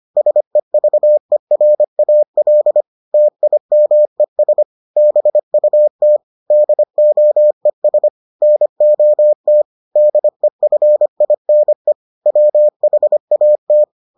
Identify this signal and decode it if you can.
Morse code, heard as a keyed tone, about 25 wpm, 600 Hz. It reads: SEVERAL TIMES BUT DOES NOT DEFINE WHAT